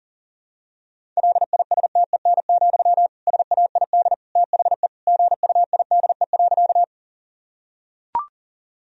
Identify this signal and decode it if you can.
Morse code, heard as a keyed tone, about 40 wpm, 700 Hz. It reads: LISTEN, SAID THE GUIDE.
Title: Sentences from Top 1000 Words - Review Part 3 - 40wpm